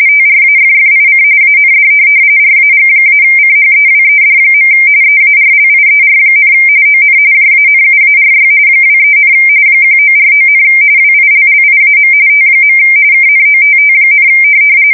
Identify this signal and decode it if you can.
RTTY, about 45.45 baud. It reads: TRAFFIC AND RAG-CHEWED. EARLIER, ON JANUARY 23, 1949, WILLIAM T. KNOTT,
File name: fsk170 / teletype-RTTY